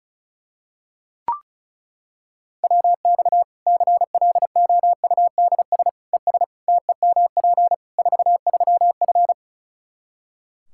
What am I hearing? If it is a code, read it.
Morse code, heard as a keyed tone, about 34 wpm, 700 Hz. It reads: WX CLOUDS ES TEMP 43F
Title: K1USN SST QSO Practice 35wpm